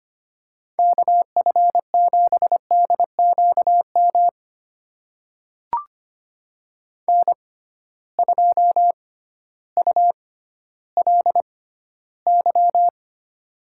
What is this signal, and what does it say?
Morse code, heard as a keyed tone, about 25 wpm, spaced as for 14 wpm, 700 Hz. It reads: KF7DQM N2ULY
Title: USA Call Signs - 6x Character Spacing - 25wpm